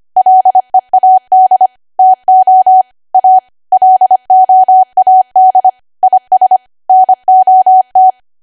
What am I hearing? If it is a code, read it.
Morse code, heard as a keyed tone, about 25 wpm, 750 Hz. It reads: LEAD TO A LOAD IS NOT
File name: cwa / W1AW-Level3 / 25WPM